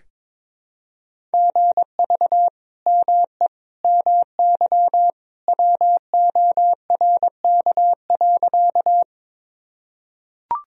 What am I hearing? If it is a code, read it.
Morse code, heard as a keyed tone, about 21 wpm, 700 Hz. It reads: GV ME MY WORK.
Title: Sentences from Top 200 Words and Abbreviations 22wpm